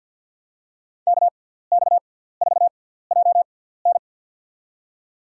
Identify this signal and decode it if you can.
Morse code, heard as a keyed tone, about 53 wpm, spaced as for 17 wpm, 700 Hz. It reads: KX4JN
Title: USA Call Signs - 5 Characters - 6x Character Spacing - 50wpm